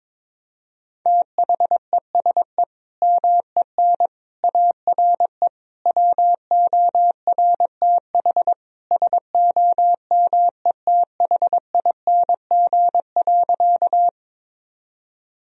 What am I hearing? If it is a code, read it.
Morse code, heard as a keyed tone, about 21 wpm, 700 Hz. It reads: THESE MEN ARE WORTH SOMETHING.
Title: Sentences from Top 1000 Words - Review Part 1 - 22wpm